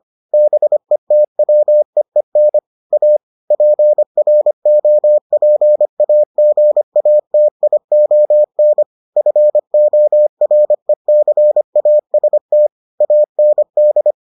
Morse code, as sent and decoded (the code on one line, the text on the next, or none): -... - .-- . -. / .- / .--. .-. --- .--. .- --. .- - .. --- -. / ..-. --- .-. -.-. .- ... - / .- -. -..
BETWEEN A PROPAGATION FORECAST AND